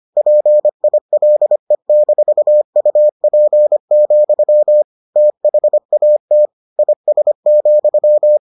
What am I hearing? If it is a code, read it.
Morse code, heard as a keyed tone, about 25 wpm, 600 Hz. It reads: PILE-UP, THAT IS,